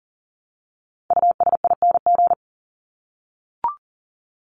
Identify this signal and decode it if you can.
Morse code, heard as a keyed tone, about 40 wpm, 700 Hz. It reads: USING
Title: ING Suffix - Top 100 Words with Base Words 40wpm